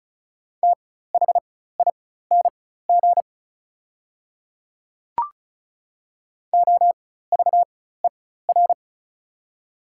Morse code, heard as a keyed tone, about 36 wpm, spaced as for 16 wpm, 690 Hz. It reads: THING OVER
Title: 4x Character Spacing - Top 100 Words - 35wpm